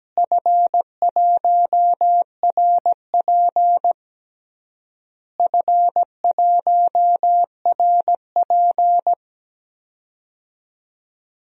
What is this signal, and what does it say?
Morse code, heard as a keyed tone, about 17 wpm, 700 Hz. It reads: F1RP F1RP
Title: DXCC-100 Call Signs - 4 Characters - Repeated 3x - 17wpm